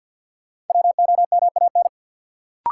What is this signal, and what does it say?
Morse code, heard as a keyed tone, about 50 wpm, 700 Hz. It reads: WOMAN
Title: Sentences from Top 1100 Words - New Word Drill Alt - 50wpm